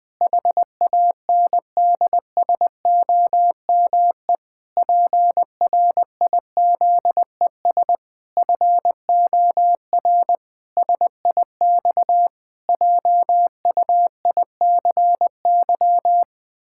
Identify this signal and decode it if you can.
Morse code, heard as a keyed tone, about 20 wpm, 700 Hz. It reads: HANDSOME PRIZES FOR SIX JUICY
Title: Pangram Challenge v2 - 20wpm